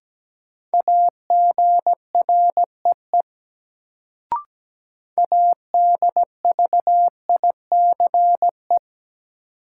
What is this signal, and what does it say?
Morse code, heard as a keyed tone, about 17 wpm, 700 Hz. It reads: AGREE ADVICE
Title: Sentences from Top 1100 Words - New Word Drill Alt - 17wpm